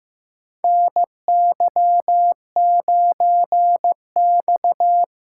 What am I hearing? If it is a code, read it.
Morse code, heard as a keyed tone, about 15 wpm, 700 Hz. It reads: NY9X
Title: K1USN SST QSO Practice - 15wpm